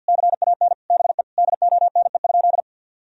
Morse code, heard as a keyed tone, about 49 wpm, 700 Hz. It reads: CAN BE DONE?
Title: Sentences from Top 1000 Words - Review Part 1 - 50wpm